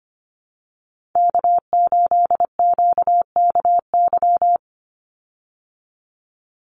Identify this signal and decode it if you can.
Morse code, heard as a keyed tone, about 25 wpm, 700 Hz. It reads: K8QKY
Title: Call Signs Repeated 2x - 25wpm